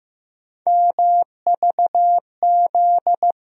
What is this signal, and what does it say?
Morse code, heard as a keyed tone, about 15 wpm, 700 Hz. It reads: MVZ
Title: USA Call Sign Postfixes - 3 Letters 15wpm